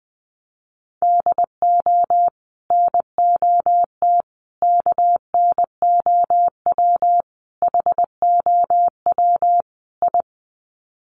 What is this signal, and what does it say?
Morse code, heard as a keyed tone, about 19 wpm, 700 Hz. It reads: DO NOT KNOW HOW I
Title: Sets of 5 Words Encore2 20wpm